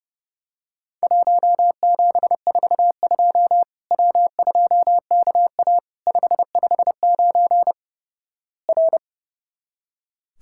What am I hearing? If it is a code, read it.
Morse code, heard as a keyed tone, about 30 wpm, 700 Hz. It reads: 1742 W2KA 559 R
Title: Worked All Europe DX Contest - QTC Exchange - 30wpm